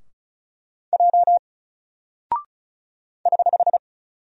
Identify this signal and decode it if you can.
Morse code, heard as a keyed tone, about 36 wpm, 700 Hz. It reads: J <HH>
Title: Single Character (Letter:Number:Punctuation:Prosign) Alt 35wpm